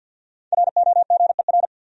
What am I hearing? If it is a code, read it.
Morse code, heard as a keyed tone, about 52 wpm, 700 Hz. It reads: ROGER
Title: USA Common Names - Max 5 Letters - 50wpm